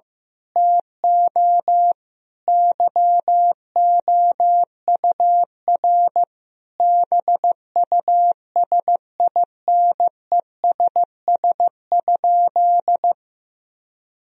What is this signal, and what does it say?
Morse code, heard as a keyed tone, about 15 wpm, 700 Hz. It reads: TO YOUR BUSINESS?
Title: Sentences from Top 1000 Words - Review Part 4 - 15wpm